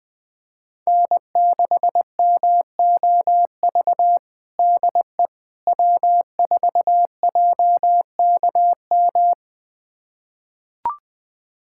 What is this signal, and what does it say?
Morse code, heard as a keyed tone, about 21 wpm, 700 Hz. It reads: N6MOV DE W4JKM
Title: USA Callsign DE Callsign - 5 Characters - 20wpm